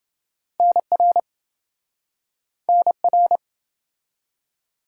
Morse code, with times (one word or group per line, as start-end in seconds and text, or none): 0.6-1.2 NR
2.7-3.4 NR